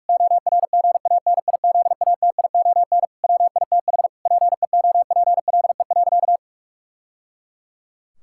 Morse code, heard as a keyed tone, about 46 wpm, 700 Hz. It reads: ORGANIZATION WITH PEOPLE.
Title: Sentences from Top 1000 Words - Review Part 5 - 45wpm